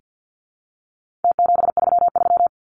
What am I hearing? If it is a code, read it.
Morse code, heard as a keyed tone, about 48 wpm, 700 Hz. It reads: T733
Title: Numbers - Contest Style - 4-Digit 50wpm